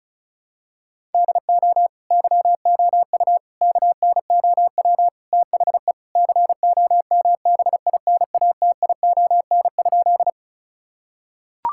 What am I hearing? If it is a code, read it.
Morse code, heard as a keyed tone, about 34 wpm, 700 Hz. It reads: DO YOU KNOW THE COMBINATION?
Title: Sentences from Top 1000 Words - Review Part 6 - 35wpm